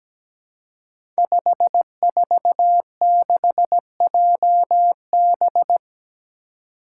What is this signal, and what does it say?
Morse code, heard as a keyed tone, about 17 wpm, 700 Hz. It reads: 546JB
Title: Five Challenging Letter-Number Combinations 17wpm